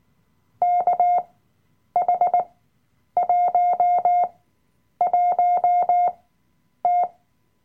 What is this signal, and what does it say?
Morse code, heard as a keyed tone, about 18 wpm, spaced as for 9 wpm, 710 Hz. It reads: KH11T
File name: Workshop Dateien / CW-Online-Kurs / Lektion 007